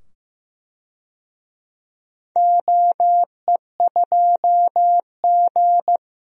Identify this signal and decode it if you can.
Morse code, heard as a keyed tone, about 15 wpm, 700 Hz. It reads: OE2G
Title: DXCC-100 Call Signs - 4 Characters - 15wpm